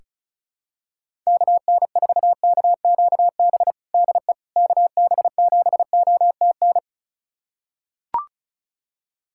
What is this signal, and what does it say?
Morse code, heard as a keyed tone, about 34 wpm, 700 Hz. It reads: KN4KQB DE KB7OTN
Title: USA Callsign DE Callsign 35wpm